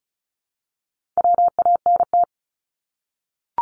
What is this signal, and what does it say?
Morse code, heard as a keyed tone, about 34 wpm, 700 Hz. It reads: WANT
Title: Top 300 Words Repeated 35wpm